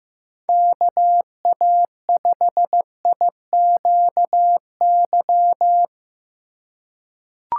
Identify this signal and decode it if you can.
Morse code, heard as a keyed tone, about 15 wpm, 700 Hz. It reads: KA5IQY
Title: USA Call Signs - 6 Characters - 4x Character Spacing - 15wpm